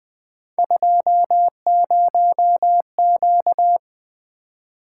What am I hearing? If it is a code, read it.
Morse code, heard as a keyed tone, about 20 wpm, 700 Hz. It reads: 20Q
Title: Three Letter-Number Combinations - 20wpm